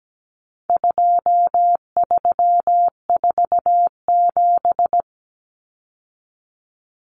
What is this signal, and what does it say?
Morse code, heard as a keyed tone, about 17 wpm, 700 Hz. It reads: 2347
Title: Numbers - 4-Digit 17wpm